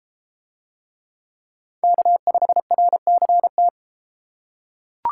Morse code, heard as a keyed tone, about 30 wpm, 700 Hz. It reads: K5RCT